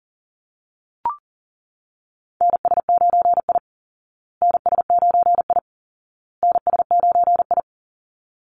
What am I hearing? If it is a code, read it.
Morse code, heard as a keyed tone, about 37 wpm, 700 Hz. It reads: NS9I NS9I NS9I
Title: Call Signs Repeated 2x - 40wpm